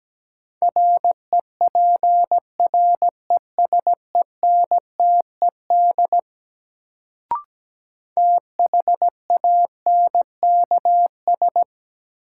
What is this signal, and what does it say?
Morse code, heard as a keyed tone, about 17 wpm, 700 Hz. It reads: REPRESENTED THANKS
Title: Sets of 5 Words - New Word Drill - 17wpm